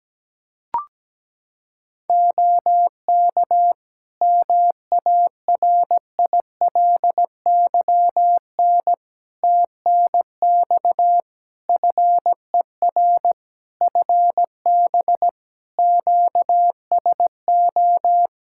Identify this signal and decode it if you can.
Morse code, heard as a keyed tone, about 17 wpm, 700 Hz. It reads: OK MARILYN TNX FER FB QSO
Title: Practice QSO - Exchange 4 - Ending - 17wpm